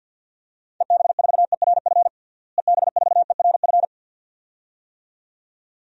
Morse code, heard as a keyed tone, about 53 wpm, 700 Hz. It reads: EB4ERF EB4ERF
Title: DXCC-100 Call Signs - 6 Characters - Repeated 2x - 50wpm